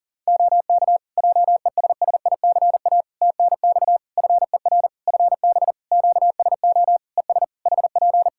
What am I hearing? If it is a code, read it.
Morse code, heard as a keyed tone, about 40 wpm, 700 Hz. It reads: OK JESSICA TNX FER FB QSO ES HP
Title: Practice QSO - Exchange 4 - Ending - 40wpm